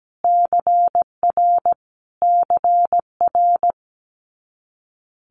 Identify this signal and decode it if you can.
Morse code, heard as a keyed tone, about 17 wpm, 700 Hz. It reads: CR CR
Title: Multi-letter Phonemes Repeated 3x 17wpm